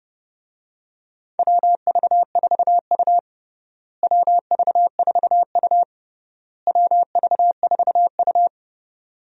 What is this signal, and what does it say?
Morse code, heard as a keyed tone, about 30 wpm, 700 Hz. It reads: WV4U WV4U WV4U
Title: USA Call Signs - 4 Characters - Repeated 3x - 30wpm